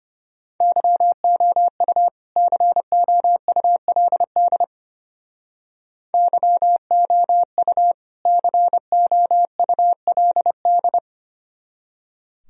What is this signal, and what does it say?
Morse code, heard as a keyed tone, about 30 wpm, 700 Hz. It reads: YOU COULD YOU COULD
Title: Speed-Racing - Sets of 2 Words 25wpm